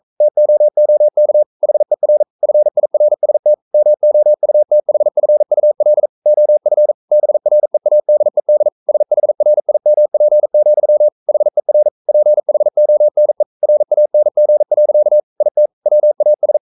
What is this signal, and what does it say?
Morse code, heard as a keyed tone, about 42 wpm, 600 Hz. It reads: TOOK HER FIRST MOUTHFUL OF BREADED SHRIMP, HER PHONE RANG. IT WAS
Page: Romanen Never af Ken Follet på engelsk.